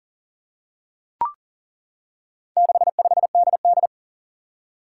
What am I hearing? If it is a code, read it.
Morse code, heard as a keyed tone, about 40 wpm, 700 Hz. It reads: 65DD
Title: Four Challenging Letter-Number Combinations 40wpm